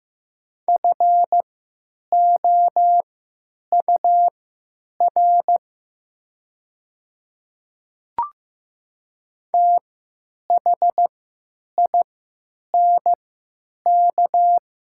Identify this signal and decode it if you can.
Morse code, heard as a keyed tone, about 15 wpm, spaced as for 8 wpm, 700 Hz. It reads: FOUR THINK
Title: 3x Character Spacing - Top 300 Words - 15wpm